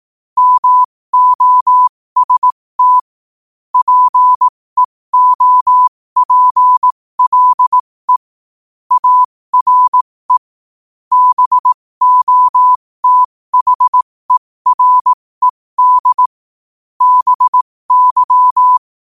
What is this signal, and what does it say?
Morse code, heard as a keyed tone, about 18 wpm, spaced as for 15 wpm, 1000 Hz. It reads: MOST PEOPLE ARE BOTHERED BY
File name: qotdfiles / 15wpm